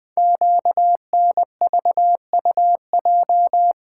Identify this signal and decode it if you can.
Morse code, heard as a keyed tone, about 20 wpm, 700 Hz. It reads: QNVUJ
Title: Code Groups - 5 Letters - 20wpm